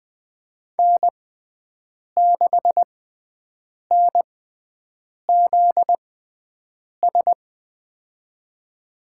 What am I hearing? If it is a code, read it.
Morse code, heard as a keyed tone, about 20 wpm, spaced as for 7 wpm, 700 Hz. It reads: N6NZS
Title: USA Call Signs - 6x Character Spacing - 20wpm